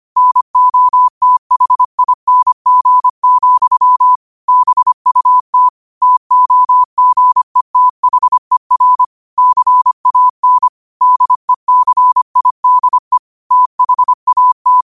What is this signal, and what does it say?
Morse code, heard as a keyed tone, about 25 wpm, 1000 Hz. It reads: NOTHING, BUT TOGETHER CAN DECIDE THAT
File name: qotdfiles / 25wpm